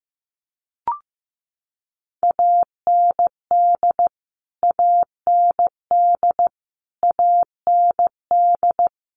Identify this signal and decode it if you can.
Morse code, heard as a keyed tone, about 15 wpm, 700 Hz. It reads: AND AND AND
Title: Top 300 Words Repeated 15wpm